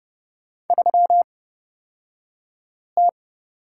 Morse code, heard as a keyed tone, about 30 wpm, spaced as for 5 wpm, 700 Hz. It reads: 3T
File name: Lesson 38d - Letter q - Two Characters 30wpm